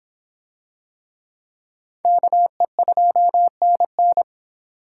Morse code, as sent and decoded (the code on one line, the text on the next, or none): -.- . ..--- -. -.
KE2NN